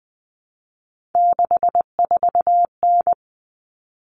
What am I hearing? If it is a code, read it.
Morse code, heard as a keyed tone, about 20 wpm, 700 Hz. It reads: 64N
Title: Common Cut Numbers - 3-Digit 20wpm